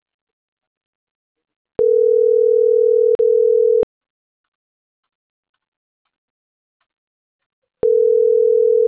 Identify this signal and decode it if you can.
Ringback tone (North America), 440 + 480 Hz.